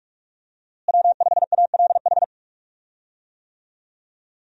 Morse code, heard as a keyed tone, about 46 wpm, 700 Hz. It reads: W5ALH
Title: USA Call Signs - 5 Characters - Repeated 3x - 45wpm